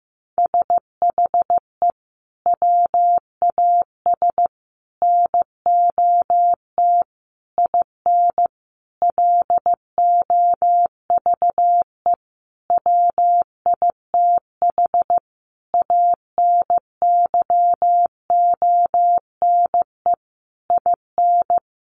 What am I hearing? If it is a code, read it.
Morse code, heard as a keyed tone, about 15 wpm, 700 Hz. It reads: SHE WAS NOT IN LOVE WITH ANYONE IN
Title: Sentences from Top 900 Words 15wpm